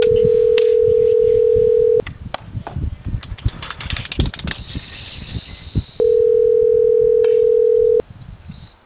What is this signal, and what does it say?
Ringback tone (North America), 440 + 480 Hz.